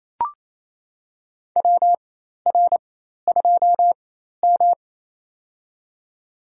Morse code, heard as a keyed tone, about 28 wpm, spaced as for 13 wpm, 700 Hz. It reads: WR2M
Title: USA Call Signs - 4x Character Spacing - 28wpm